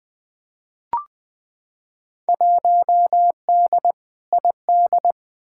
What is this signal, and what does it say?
Morse code, heard as a keyed tone, about 20 wpm, 700 Hz. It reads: 1D ID
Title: Field Day Exchanges - 20wpm